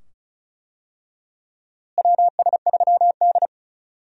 Morse code, heard as a keyed tone, about 36 wpm, 700 Hz. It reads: WS3D
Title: USA Call Signs - 4 Characters - Repeated 2x - 35wpm